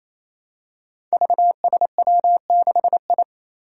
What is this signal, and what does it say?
Morse code, heard as a keyed tone, about 27 wpm, 700 Hz. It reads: VSW6I
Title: Five Challenging Letter-Number Combinations 28wpm